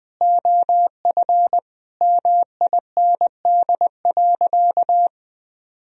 Morse code, heard as a keyed tone, about 21 wpm, 700 Hz. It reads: OF MIND.
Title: Sentences from Top 1000 Words - Review Part 2 - 20wpm